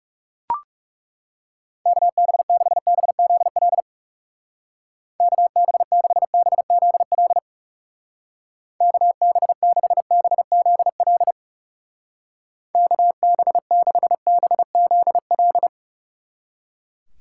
Morse code, heard as a keyed tone, about 37 wpm, 700 Hz. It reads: KB6BZL KB6BZL KB6BZL KB6BZL
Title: Speed Racing - USA Call Signs 30wpm